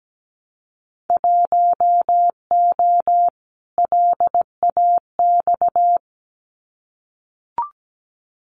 Morse code, heard as a keyed tone, about 17 wpm, 700 Hz. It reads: 1O LAX
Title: Winter Field Day Exchanges 17wpm